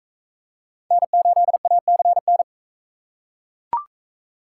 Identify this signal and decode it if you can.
Morse code, heard as a keyed tone, about 42 wpm, 700 Hz. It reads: N8ACN